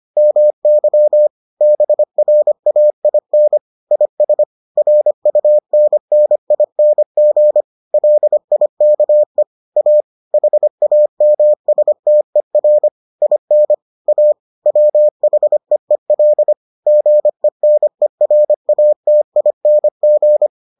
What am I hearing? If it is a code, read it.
Morse code, heard as a keyed tone, about 25 wpm, 600 Hz. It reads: MY BRAIN IS RUNNING LIKE A HAMSTER IN A WHEEL GENERATING